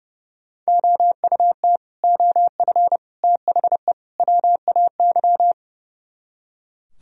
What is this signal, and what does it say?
Morse code, heard as a keyed tone, about 38 wpm, 700 Hz. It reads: OUT OF THE WAY